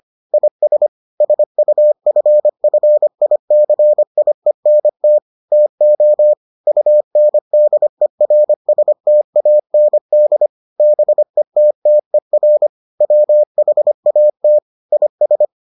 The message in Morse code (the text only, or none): IS SUFFICIENT TO UNDERSTAND BETTER WHAT IS